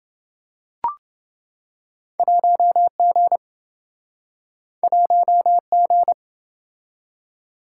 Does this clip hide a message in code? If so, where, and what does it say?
Yes: Morse code from 2.2 to 6.1 s, heard as a keyed tone, about 28 wpm, 700 Hz. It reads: 1G 1G